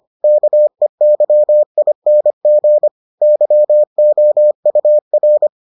Morse code, heard as a keyed tone, about 25 wpm, 600 Hz. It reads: KEYING YOUR